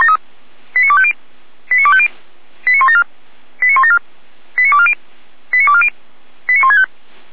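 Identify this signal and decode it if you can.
Five-tone selective call (ZVEI), heard as quick runs of five tones, tones of about 70 ms. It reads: 89250 89250 89174 89174 89240 89240 89165
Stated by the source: ZVEI five-tone selective call received with an SDR, from feuerwehr-orlamuende.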